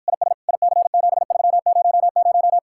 Morse code, heard as a keyed tone, about 52 wpm, 700 Hz. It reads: IS IC7300